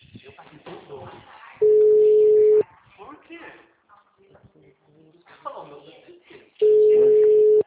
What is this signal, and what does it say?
Ringback tone (Europe, 425 Hz), with speech.
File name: out- -2020- - - .86378